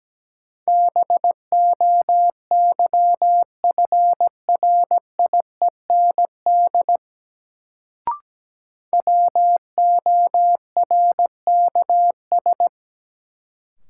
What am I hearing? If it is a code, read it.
Morse code, heard as a keyed tone, about 17 wpm, 700 Hz. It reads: BOYFRIEND WORKS
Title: Sentences from Top 1000 Words - Review Part 5 - New Word Drill - 17wpm